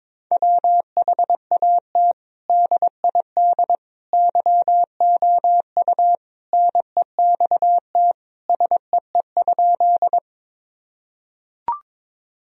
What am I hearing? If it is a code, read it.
Morse code, heard as a keyed tone, about 22 wpm, 700 Hz. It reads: WHAT DID YOU NEXT SEE?
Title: Sentences from Top 1000 Words - Review Part 1 - 22wpm